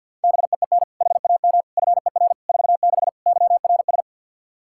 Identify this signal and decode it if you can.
Morse code, heard as a keyed tone, about 51 wpm, 700 Hz. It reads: BEEN HAM FER 46 YRS
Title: K1USN SST QSO Practice 50wpm